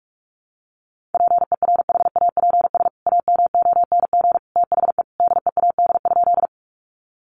Morse code, heard as a keyed tone, about 44 wpm, 700 Hz. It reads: PERHAPS AMONG THE DEAD?
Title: Sentences from Top 1500 Words 45wpm